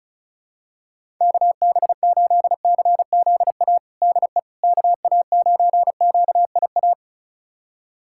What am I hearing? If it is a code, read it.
Morse code, heard as a keyed tone, about 36 wpm, 700 Hz. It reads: KB8CZA DE KA9QIA
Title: USA Callsign DE Callsign 35wpm